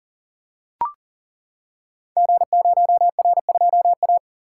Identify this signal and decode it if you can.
Morse code, heard as a keyed tone, about 40 wpm, 700 Hz. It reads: G0R2A
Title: Code Groups - 5 Letter:Numbers - 40wpm